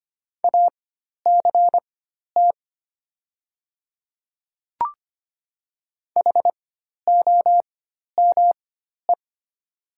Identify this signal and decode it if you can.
Morse code, heard as a keyed tone, about 25 wpm, spaced as for 12 wpm, 700 Hz. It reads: ACT HOME
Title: 4x Character Spacing - Top 300 Words - 25wpm